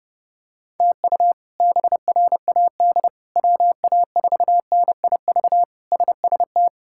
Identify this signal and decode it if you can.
Morse code, heard as a keyed tone, about 32 wpm, 700 Hz. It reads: TU BRAD WA4NIV SST
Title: K1USN SST QSO Practice - 30wpm